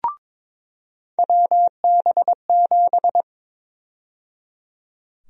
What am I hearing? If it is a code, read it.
Morse code, heard as a keyed tone, about 22 wpm, 700 Hz. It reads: WB7
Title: USA Call Sign Prefixes - 22wpm